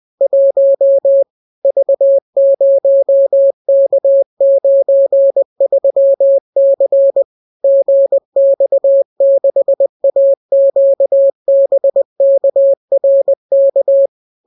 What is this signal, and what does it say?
Morse code, heard as a keyed tone, about 20 wpm, 550 Hz. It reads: 1 V0K93C GX6AQBKRK